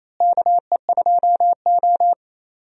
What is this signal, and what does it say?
Morse code, heard as a keyed tone, about 28 wpm, 700 Hz. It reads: KE2O
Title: USA Call Signs - 4 Characters - Repeated 2x - 28wpm